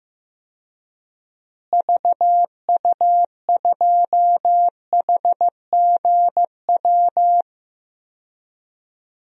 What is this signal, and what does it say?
Morse code, heard as a keyed tone, about 15 wpm, 700 Hz. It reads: VU2HGW
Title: DXCC-100 Call Signs - 6 Characters - Repeated 2x - 15wpm